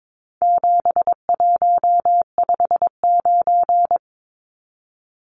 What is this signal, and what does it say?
Morse code, heard as a keyed tone, about 22 wpm, 700 Hz. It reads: 7159
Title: Numbers - 4-Digit 22wpm